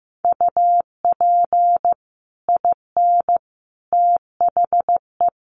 Morse code, heard as a keyed tone, about 15 wpm, 700 Hz. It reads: UP IN THE
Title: Sets of 5 Words Encore 15wpm